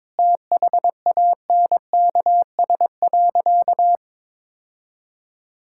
Morse code, heard as a keyed tone, about 22 wpm, 700 Hz. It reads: THANKS.
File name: practice / 22wpm / Sentences with 2-Word Phrases - 22wpm